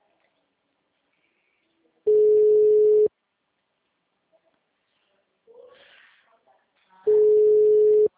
Ringback tone (Europe, 425 Hz).
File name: out- -2012- - - .1173